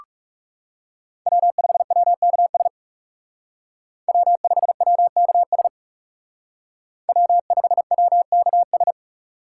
Morse code, heard as a keyed tone, about 39 wpm, 700 Hz. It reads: W5WKS W5WKS W5WKS